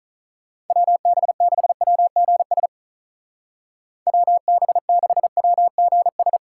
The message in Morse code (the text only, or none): WB6WGS WB6WGS